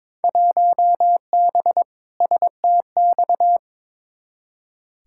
Morse code, heard as a keyed tone, about 22 wpm, 700 Hz. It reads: 1B STX
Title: Field Day Exchanges Encore - 22wpm